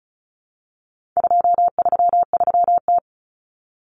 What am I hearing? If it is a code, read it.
Morse code, heard as a keyed tone, about 34 wpm, 700 Hz. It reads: 233T
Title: Common Cut Numbers - 4-Digit 35wpm